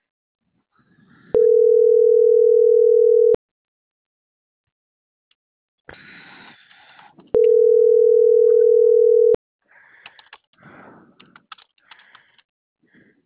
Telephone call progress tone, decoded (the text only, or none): ringback tone